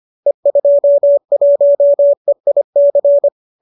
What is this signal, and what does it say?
Morse code, heard as a keyed tone, about 25 wpm, 570 Hz. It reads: E21EIC